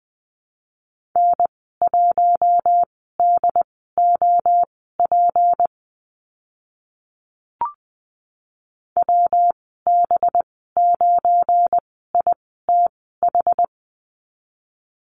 Morse code, heard as a keyed tone, about 20 wpm, spaced as for 14 wpm, 700 Hz. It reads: N1DOP WB9ITH
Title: 186 Lesson 30g - Letter b - Call Signs 20wpm